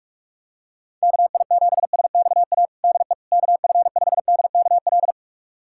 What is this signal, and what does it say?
Morse code, heard as a keyed tone, about 42 wpm, 700 Hz. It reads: KI7SXA DE KF5DKL
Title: USA Callsign DE Callsign - 6 Characters - 45wpm